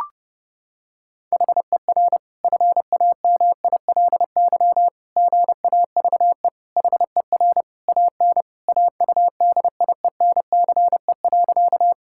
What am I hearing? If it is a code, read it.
Morse code, heard as a keyed tone, about 32 wpm, 700 Hz. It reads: HER FAMILY GAVE HER AN AUDIENCE.
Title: Sentences from Top 1000 Words - Review Part 5 - 30wpm